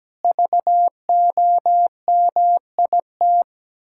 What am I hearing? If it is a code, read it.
Morse code, heard as a keyed tone, about 17 wpm, 700 Hz. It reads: VOMIT
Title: Pangram Challenge v2 - 17wpm